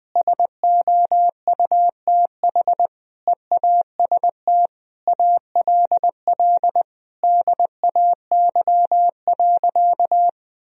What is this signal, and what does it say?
Morse code, heard as a keyed tone, about 20 wpm, 700 Hz. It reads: SOUTH EAST ALL DAY.
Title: Sentences from Top 1000 Words - Review Part 3 - 20wpm